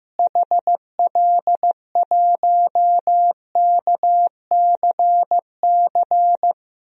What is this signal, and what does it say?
Morse code, heard as a keyed tone, about 15 wpm, 700 Hz. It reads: HL1KCC
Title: DXCC-100 Call Signs - 6 Characters - Repeated 3x - 15wpm